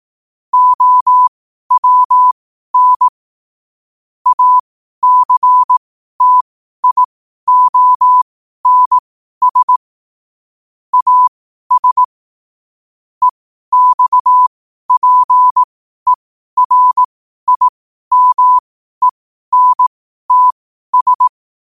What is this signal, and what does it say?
Morse code, heard as a keyed tone, about 18 wpm, spaced as for 12 wpm, 1000 Hz. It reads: OWN ACTIONS AS EXPERIMENTS
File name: qotdfiles / 12wpm